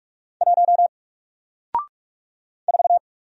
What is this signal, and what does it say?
Morse code, heard as a keyed tone, about 44 wpm, 700 Hz. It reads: 1 4
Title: Single Character (Letter:Number:Punctuation:Prosign) Alt 45wpm